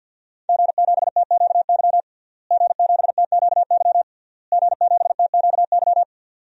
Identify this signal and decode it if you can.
Morse code, heard as a keyed tone, about 52 wpm, 700 Hz. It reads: G7TQY G7TQY G7TQY
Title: DXCC-100 Call Signs - 5 Characters - Repeated 3x - 50wpm